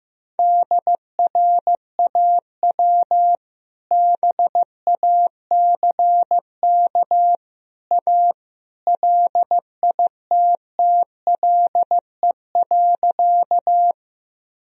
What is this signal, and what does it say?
Morse code, heard as a keyed tone, about 15 wpm, 700 Hz. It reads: DRAW BACK A LITTLE.
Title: Sentences from Top 1000 Words - Review Part 1 - 15wpm